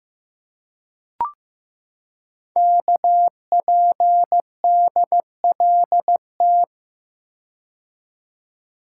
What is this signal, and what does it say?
Morse code, heard as a keyed tone, about 15 wpm, 700 Hz. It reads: KPDLT
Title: Code Groups - 5 Letters - 15wpm